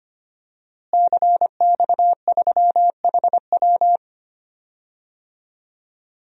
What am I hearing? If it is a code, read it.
Morse code, heard as a keyed tone, about 25 wpm, 700 Hz. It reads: CX3HW
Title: DXCC-100 Call Signs - 5 Characters - Repeated 3x - 25wpm